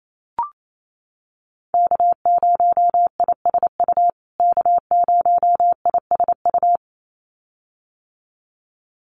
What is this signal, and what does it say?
Morse code, heard as a keyed tone, about 28 wpm, 700 Hz. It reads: K0ISU K0ISU
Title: USA Call Signs Repeated 2x 28wpm